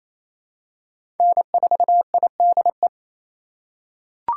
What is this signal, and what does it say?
Morse code, heard as a keyed tone, about 27 wpm, 700 Hz. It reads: N4IDE
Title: USA Call Signs - 3x Character Spacing - 28wpm